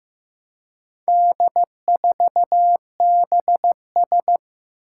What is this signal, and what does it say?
Morse code, heard as a keyed tone, about 15 wpm, 700 Hz. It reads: D4BS
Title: Four Challenging Letter-Number Combinations 15wpm